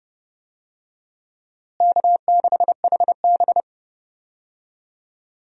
Morse code, heard as a keyed tone, about 31 wpm, 700 Hz. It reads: K6HB
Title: USA Call Signs - 4 Characters - Repeated 2x - 30wpm